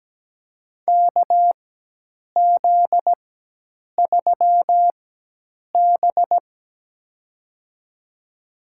Morse code, heard as a keyed tone, about 17 wpm, spaced as for 8 wpm, 700 Hz. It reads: KZ3B
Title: USA Call Signs - 4 Characters - 4x Character Spacing - 17wpm